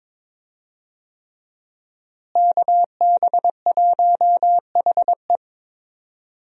Morse code, heard as a keyed tone, about 22 wpm, 700 Hz. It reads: KB1HE
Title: USA Call Signs - 5 Characters - Repeated 3x - 22wpm